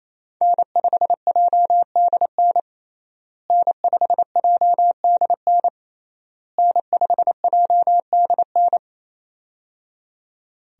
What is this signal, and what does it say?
Morse code, heard as a keyed tone, about 28 wpm, 700 Hz. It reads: N5JDN N5JDN N5JDN
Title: USA Call Signs - 5 Characters - Repeated 3x - 28wpm